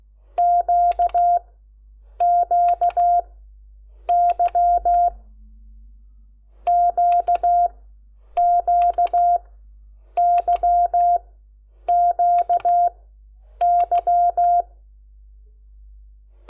Morse code, heard as a keyed tone, about 16 wpm, 700 Hz. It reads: Q Q Y Q Q Y Q Y
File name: Lecciones / CLASE Nº10